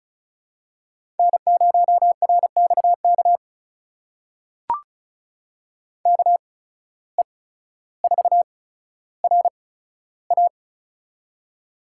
Morse code, heard as a keyed tone, about 34 wpm, spaced as for 15 wpm, 700 Hz. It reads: N0RXK KE4RA
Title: USA Call Signs - 5 Characters - 8x Character Spacing - 35wpm